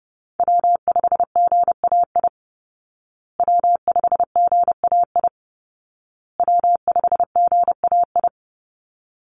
Morse code, heard as a keyed tone, about 28 wpm, 700 Hz. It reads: W5GAI W5GAI W5GAI
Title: Call Signs Repeated 2x - 30wpm